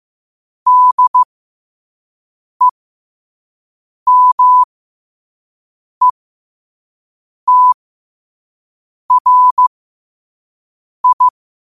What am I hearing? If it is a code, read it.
Morse code, heard as a keyed tone, about 15 wpm, spaced as for 5 wpm, 1000 Hz. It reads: DEMETRI
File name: qotdfiles / 5wpm